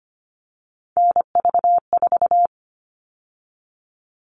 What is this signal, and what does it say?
Morse code, heard as a keyed tone, about 25 wpm, 700 Hz. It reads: NV4
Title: All Cut Numbers - 3-Digit 25wpm